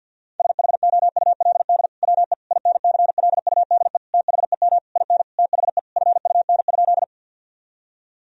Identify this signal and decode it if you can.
Morse code, heard as a keyed tone, about 48 wpm, 700 Hz. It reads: SHOULD WE INCLUDE THEM IN THE FUN?
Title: Sentences from Top 1000 Words - Review Part 3 - 50wpm